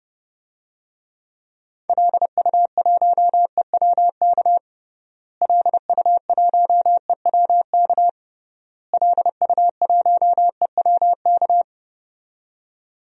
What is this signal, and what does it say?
Morse code, heard as a keyed tone, about 30 wpm, 700 Hz. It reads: LU1EWK LU1EWK LU1EWK
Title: DXCC-100 Call Signs - 6 Characters - Repeated 3x - 30wpm